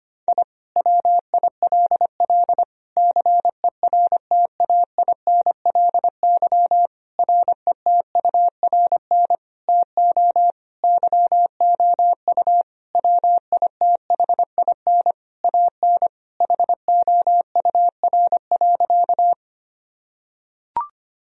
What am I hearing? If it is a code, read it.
Morse code, heard as a keyed tone, about 25 wpm, 700 Hz. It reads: I WILL CERTAINLY RETURN TO YOU WITHIN AN HOUR.
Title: Sentences from Top 1000 Words - Review Part 4 - 25wpm